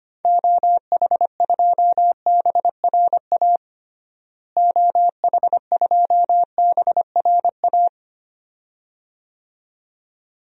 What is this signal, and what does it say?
Morse code, heard as a keyed tone, about 25 wpm, 700 Hz. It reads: OH2BRA OH2BRA
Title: DXCC-100 Call Signs - 6 Characters - Repeated 2x - 25wpm